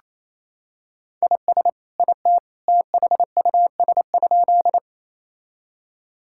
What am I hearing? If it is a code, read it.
Morse code, heard as a keyed tone, about 28 wpm, 700 Hz. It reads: IS IT THUS?
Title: Sentences from Top 1000 Words - Review Part 5 - 28wpm